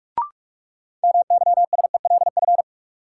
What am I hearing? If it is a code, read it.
Morse code, heard as a keyed tone, about 41 wpm, 700 Hz. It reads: MYSELF
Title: Sentences from Top 1300 Words - New Word Drill Alt - 45wpm